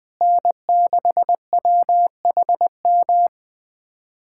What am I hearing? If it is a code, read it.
Morse code, heard as a keyed tone, about 21 wpm, 700 Hz. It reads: N6WHM
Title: USA Callsign DE Callsign - 5 Characters - 20wpm